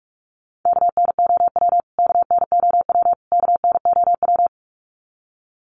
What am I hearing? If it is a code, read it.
Morse code, heard as a keyed tone, about 44 wpm, 700 Hz. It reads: KNOW KNOW KNOW
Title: Top 300 Words Repeated 45wpm